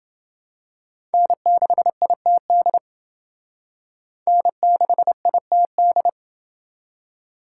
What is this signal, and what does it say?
Morse code, heard as a keyed tone, about 28 wpm, 700 Hz. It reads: N6ITD N6ITD